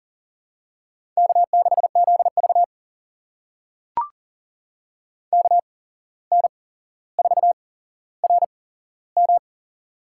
Morse code, heard as a keyed tone, about 40 wpm, spaced as for 16 wpm, 700 Hz. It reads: K6ZV KN4RM
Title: USA Call Signs - 8x Character Spacing - 40wpm